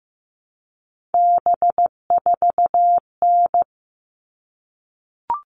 Morse code, heard as a keyed tone, about 15 wpm, 700 Hz. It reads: B4N
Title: All Cut Numbers - 3-Digit 15wpm